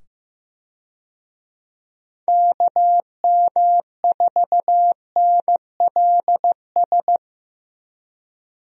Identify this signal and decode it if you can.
Morse code, heard as a keyed tone, about 15 wpm, 700 Hz. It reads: KM4NLS
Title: USA Call Signs - 6 Characters - 15wpm